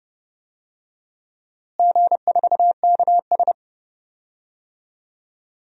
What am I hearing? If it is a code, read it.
Morse code, heard as a keyed tone, about 32 wpm, 700 Hz. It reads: G4KS
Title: DXCC-100 Call Signs - 4 Characters - Repeated 2x - 30wpm